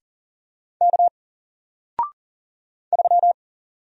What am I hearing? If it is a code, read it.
Morse code, heard as a keyed tone, about 40 wpm, 700 Hz. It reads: K 3
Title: Single Character (Letter:Number:Punctuation:Prosign) Alt 40wpm